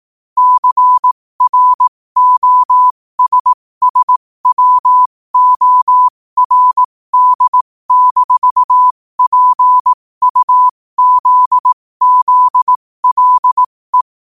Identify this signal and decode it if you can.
Morse code, heard as a keyed tone, about 18 wpm, spaced as for 15 wpm, 1000 Hz. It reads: CROSSWORD-PUZZLE
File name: qotdfiles / 15wpm